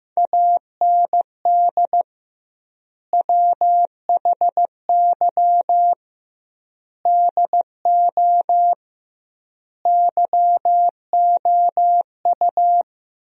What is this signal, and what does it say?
Morse code, heard as a keyed tone, about 15 wpm, 700 Hz. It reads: AND WHY DO YOU
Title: 2x Word-Spacing - Sentences from Top 400 Words - 15wpm